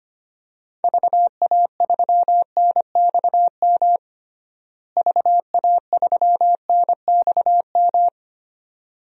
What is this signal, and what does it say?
Morse code, heard as a keyed tone, about 25 wpm, 700 Hz. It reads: VA3NXM VA3NXM
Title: DXCC-100 Call Signs - 6 Characters - Repeated 2x - 25wpm